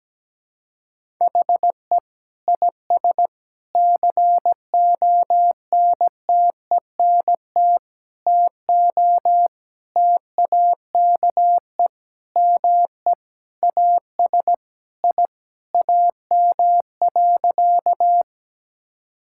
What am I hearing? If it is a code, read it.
Morse code, heard as a keyed tone, about 17 wpm, 700 Hz. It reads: HE IS CONTENT TO TAKE ME AS I AM.
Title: Sentences from Top 1000 Words - Review Part 6 - 17wpm